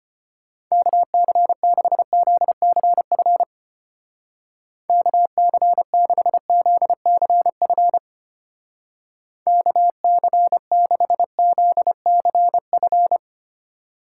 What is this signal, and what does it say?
Morse code, heard as a keyed tone, about 31 wpm, 700 Hz. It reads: KC6ZCF KC6ZCF KC6ZCF
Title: Speed Racing - USA 2x3 Call Signs 25wpm